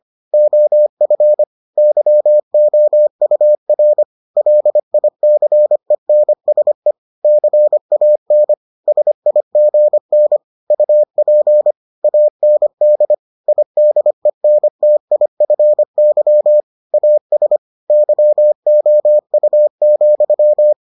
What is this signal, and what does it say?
Morse code, heard as a keyed tone, about 25 wpm, 600 Hz. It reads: OF YOUR LICENSE CAN SIGN UP AND IDENTIFY AS YOU,